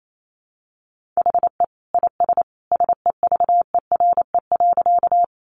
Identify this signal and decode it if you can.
Morse code, heard as a keyed tone, about 28 wpm, 700 Hz. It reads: HE IS SEVERE.
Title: Sentences from Top 1400 Words 28wpm